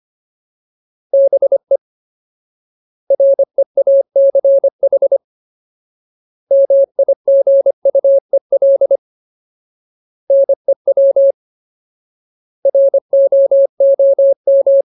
Morse code, heard as a keyed tone, about 25 wpm, 550 Hz. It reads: BE REACH MIGUEL NEW ROOM
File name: Ultimate Word List / 25WPM Ultimate